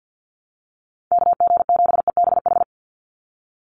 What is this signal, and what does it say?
Morse code, heard as a keyed tone, about 48 wpm, 700 Hz. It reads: KG7EBH
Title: USA Call Signs 50wpm